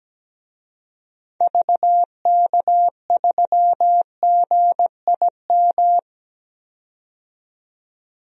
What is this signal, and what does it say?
Morse code, heard as a keyed tone, about 17 wpm, 700 Hz. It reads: VK3GIM
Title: DXCC-100 Call Signs - 6 Characters - 17wpm